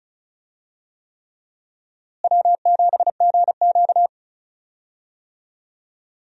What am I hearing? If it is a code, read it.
Morse code, heard as a keyed tone, about 34 wpm, 700 Hz. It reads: W7GQ
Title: USA Call Signs - 4 Characters - Repeated 3x - 35wpm